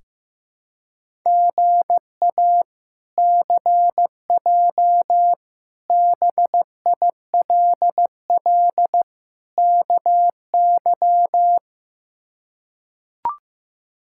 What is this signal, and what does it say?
Morse code, heard as a keyed tone, about 15 wpm, 700 Hz. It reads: GA CJ BILL KY
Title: K1USN SST QSO Practice - 15wpm